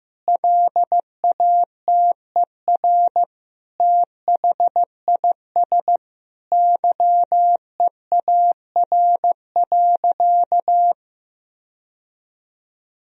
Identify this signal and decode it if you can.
Morse code, heard as a keyed tone, about 16 wpm, 700 Hz. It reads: LATER THIS YEAR.
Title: Sentences from Top 1000 Words - Review Part 3 - 15wpm